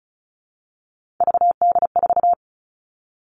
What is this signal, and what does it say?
Morse code, heard as a keyed tone, about 34 wpm, 700 Hz. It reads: VD4
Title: All Cut Numbers - 3-Digit 35wpm